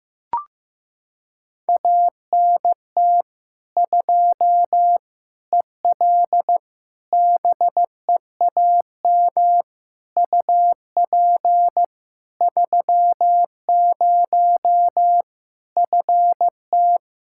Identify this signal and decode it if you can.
Morse code, heard as a keyed tone, about 15 wpm, 700 Hz. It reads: ANT 2 EL BEAM UP 30 FT
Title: Practice QSO - Exchange 2 - Info - 15wpm